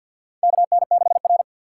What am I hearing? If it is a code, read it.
Morse code, heard as a keyed tone, about 49 wpm, 700 Hz. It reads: KN6R
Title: USA Call Signs - 6x Character Spacing - 50wpm